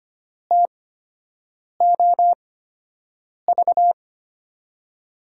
Morse code, heard as a keyed tone, about 25 wpm, spaced as for 7 wpm, 700 Hz. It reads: TOV
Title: USA Call Signs - 5 Characters - 8x Character Spacing - 25wpm